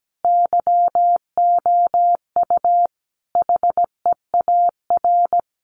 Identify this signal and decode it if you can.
Morse code, heard as a keyed tone, about 17 wpm, 700 Hz. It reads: YOU HEAR
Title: Sentences from Top 300 Words 17wpm